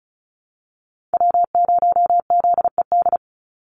Morse code, heard as a keyed tone, about 34 wpm, 700 Hz. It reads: W0ZED
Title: USA Call Signs 35wpm